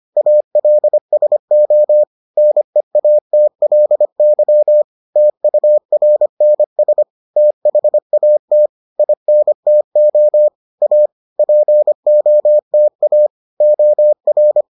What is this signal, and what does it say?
Morse code, heard as a keyed tone, about 25 wpm, 600 Hz. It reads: ALSO NEATLY TURNS THAT INTO A POTA OR